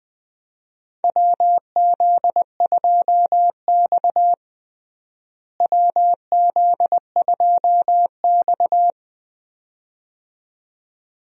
Morse code, heard as a keyed tone, about 20 wpm, 700 Hz. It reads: WZ2X WZ2X
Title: USA Call Signs - 4 Characters - Repeated 2x - 20wpm